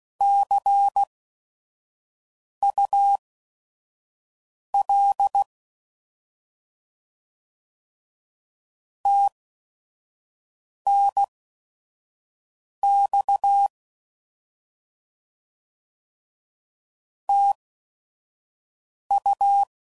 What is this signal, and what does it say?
Morse code, heard as a keyed tone, about 16 wpm, spaced as for 5 wpm, 800 Hz. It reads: CUL TNX TU